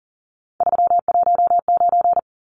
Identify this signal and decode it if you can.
Morse code, heard as a keyed tone, about 40 wpm, 700 Hz. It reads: 319
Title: Numbers - 3-Digit 40wpm